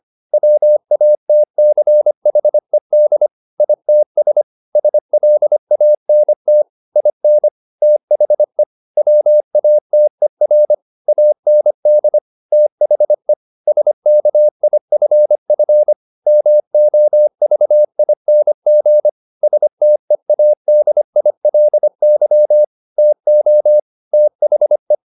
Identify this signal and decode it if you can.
Morse code, heard as a keyed tone, about 25 wpm, 600 Hz. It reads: WATCHED ITS SLANT IN THE WATER AND THE SKIFF MOVING STEADILY TO THE
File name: the old man and the sea part 1 25wpm